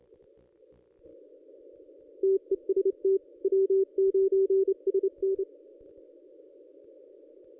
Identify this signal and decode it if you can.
Morse code, heard as a keyed tone, about 33 wpm, 380 Hz. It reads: TEST W9SN
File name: contests / ss-cw-k5tr-op-n5rz -1400